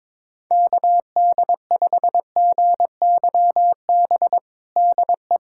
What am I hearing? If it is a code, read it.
Morse code, heard as a keyed tone, about 21 wpm, 700 Hz. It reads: KD5GYB DE
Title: USA Callsign DE Callsign - 6 Characters - 22wpm